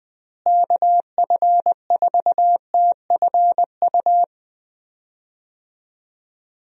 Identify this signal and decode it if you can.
Morse code, heard as a keyed tone, about 20 wpm, 700 Hz. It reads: KF4TFU
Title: USA Call Signs - 6 Characters - Repeated 3x - 20wpm